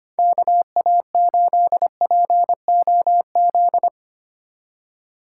Morse code, heard as a keyed tone, about 25 wpm, 700 Hz. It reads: KA8POZ
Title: K1USN SST QSO Practice - 25wpm